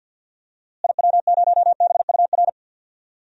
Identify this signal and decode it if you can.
Morse code, heard as a keyed tone, about 47 wpm, 700 Hz. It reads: IW0BUR
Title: DXCC-100 Call Signs - 6 Characters - Repeated 3x - 50wpm